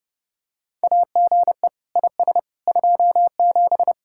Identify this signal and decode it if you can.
Morse code, heard as a keyed tone, about 32 wpm, 700 Hz. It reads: AGE IS 27